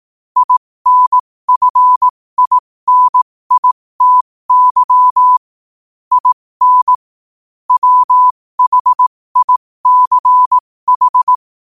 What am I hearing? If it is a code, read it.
Morse code, heard as a keyed tone, about 18 wpm, spaced as for 15 wpm, 1000 Hz. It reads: INFINITY IN WHICH
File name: qotdfiles / 15wpm